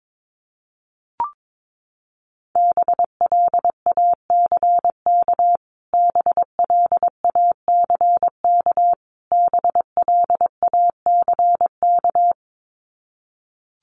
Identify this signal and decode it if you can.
Morse code, heard as a keyed tone, about 22 wpm, 700 Hz. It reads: BLACK BLACK BLACK
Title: Top 500 Words Repeated 22wpm